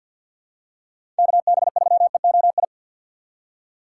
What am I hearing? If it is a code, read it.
Morse code, heard as a keyed tone, about 53 wpm, 700 Hz. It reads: KB3EOI
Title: USA Call Signs - 6 Characters - 8x Character Spacing - 50wpm